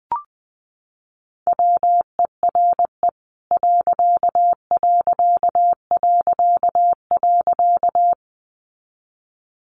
Morse code, heard as a keyed tone, about 20 wpm, 700 Hz. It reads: WERE ....
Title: A Sound Of Thunder 20wpm